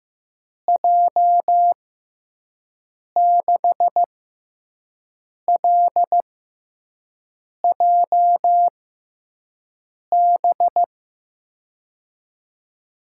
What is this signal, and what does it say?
Morse code, heard as a keyed tone, about 15 wpm, spaced as for 5 wpm, 700 Hz. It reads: J6LJB
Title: USA Call Signs - 6 Characters - 6x Character Spacing - 15wpm